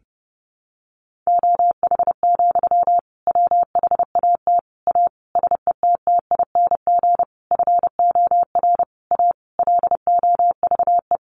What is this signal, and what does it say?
Morse code, heard as a keyed tone, about 30 wpm, 700 Hz. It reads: OH, WHAT A SETTING FOR A LOVE
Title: Sentences from Top 1300 Words 30wpm